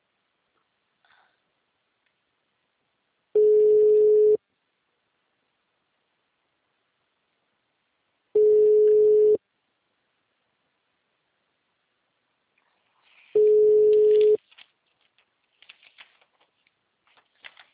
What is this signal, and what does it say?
Ringback tone (Europe, 425 Hz).
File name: out- -2034- - - .897